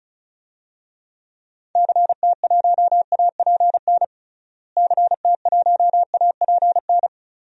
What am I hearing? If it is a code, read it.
Morse code, heard as a keyed tone, about 34 wpm, 700 Hz. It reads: CT1APN CT1APN
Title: DXCC-100 Call Signs - 6 Characters - Repeated 2x - 35wpm